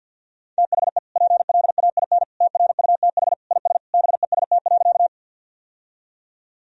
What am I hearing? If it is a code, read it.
Morse code, heard as a keyed tone, about 51 wpm, 700 Hz. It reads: THE PLAIN TRUTH IS BEST.
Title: Sentences from Top 1000 Words - Review Part 2 - 50wpm